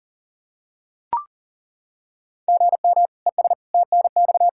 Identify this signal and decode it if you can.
Morse code, heard as a keyed tone, about 40 wpm, 700 Hz. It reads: GM ES TNX
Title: Practice QSO - Exchange 1 - Beginning - 40wpm